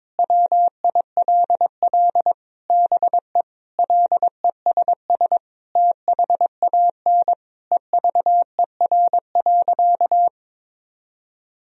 Morse code, heard as a keyed tone, about 22 wpm, 700 Hz. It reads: WILL BE LESS THAN EVER.
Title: Sentences from Top 1000 Words - Review Part 2 - 22wpm